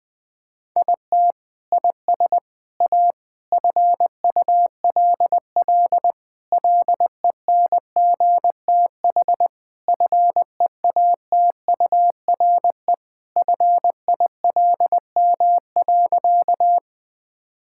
Morse code, heard as a keyed tone, about 20 wpm, 700 Hz. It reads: IT IS A FULL LENGTH FEATURE FILM.
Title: Sentences from Top 1000 Words - Review Part 5 - 20wpm